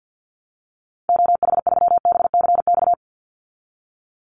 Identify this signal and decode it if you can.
Morse code, heard as a keyed tone, about 50 wpm, 700 Hz. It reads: OH3BCX
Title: Call Signs Repeated 2x - 50wpm